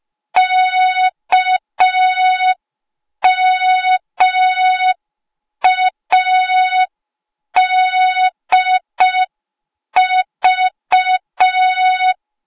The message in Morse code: KMADV